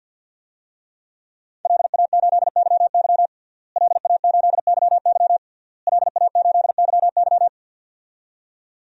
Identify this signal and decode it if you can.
Morse code, heard as a keyed tone, about 51 wpm, 700 Hz. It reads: LA8YY LA8YY LA8YY
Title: DXCC-100 Call Signs - 5 Characters - Repeated 3x - 50wpm